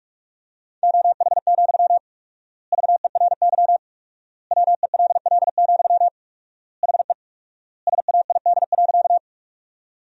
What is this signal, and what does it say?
Morse code, heard as a keyed tone, about 46 wpm, 700 Hz. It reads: OH, VERY WELL, HE SAID.
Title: 4x Word-Spacing - Sentences from Top 500 Words - 45wpm